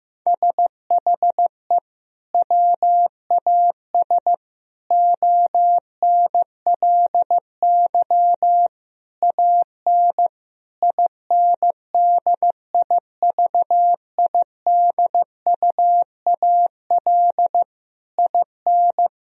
Morse code, heard as a keyed tone, about 16 wpm, 700 Hz. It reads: SHE WAS ONLY AN INDIVIDUAL IN
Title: Sentences from Top 1000 Words - Review Part 4 - 15wpm